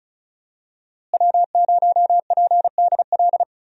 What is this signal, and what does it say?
Morse code, heard as a keyed tone, about 34 wpm, 700 Hz. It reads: W0PDL
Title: USA Call Signs - 5 Characters - Repeated 3x - 35wpm